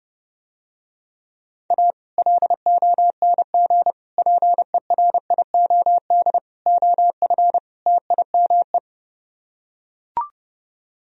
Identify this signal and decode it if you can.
Morse code, heard as a keyed tone, about 30 wpm, 700 Hz. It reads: A LONG PERIOD OF TIME